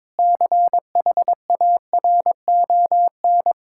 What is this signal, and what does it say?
Morse code, heard as a keyed tone, about 22 wpm, 700 Hz. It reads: CHARON
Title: Pangram Challenge v2 - 22wpm